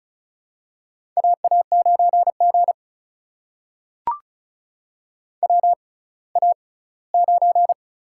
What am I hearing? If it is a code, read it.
Morse code, heard as a keyed tone, about 36 wpm, spaced as for 20 wpm, 700 Hz. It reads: AA9G WA9
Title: USA Call Signs - 4 Characters - 6x Character Spacing - 35wpm